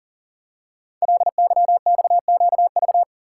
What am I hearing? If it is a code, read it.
Morse code, heard as a keyed tone, about 42 wpm, 700 Hz. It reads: LYXQV
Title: Code Groups - 5 Letters - 40wpm